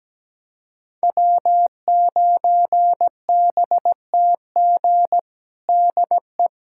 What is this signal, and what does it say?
Morse code, heard as a keyed tone, about 17 wpm, 700 Hz. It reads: W9BTG DE
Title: USA Callsign DE Callsign - 5 Characters - 17wpm